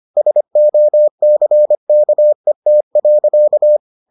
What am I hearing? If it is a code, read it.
Morse code, heard as a keyed tone, about 25 wpm, 600 Hz. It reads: SOCKET.